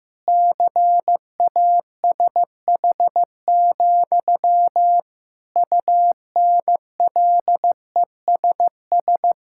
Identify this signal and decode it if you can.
Morse code, heard as a keyed tone, about 15 wpm, 700 Hz. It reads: CASH, UNLESS
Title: Sentences from Top 1000 Words - Review Part 6 - 15wpm